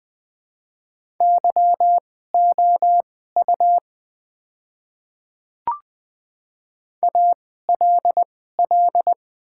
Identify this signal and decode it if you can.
Morse code, heard as a keyed tone, about 20 wpm, spaced as for 15 wpm, 700 Hz. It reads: YOU ALL
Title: 2x Character Spacing - Top 300 Words - 20wpm